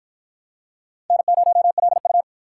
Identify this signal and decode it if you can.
Morse code, heard as a keyed tone, about 48 wpm, 700 Hz. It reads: N0LU